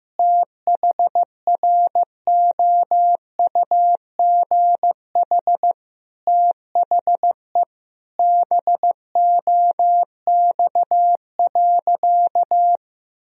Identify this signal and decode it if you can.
Morse code, heard as a keyed tone, about 16 wpm, 700 Hz. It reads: THROUGH THE BOX.
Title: Sentences from Top 1000 Words - Review Part 2 - 15wpm